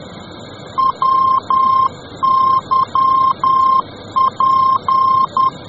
Morse code, heard as a keyed tone, about 10 wpm, 1060 Hz. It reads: WYP